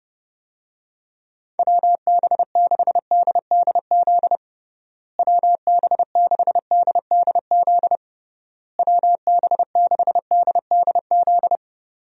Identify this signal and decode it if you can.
Morse code, heard as a keyed tone, about 30 wpm, 700 Hz. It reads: WB6DDZ WB6DDZ WB6DDZ
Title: USA Call Signs - 6 Characters - Repeated 3x - 30wpm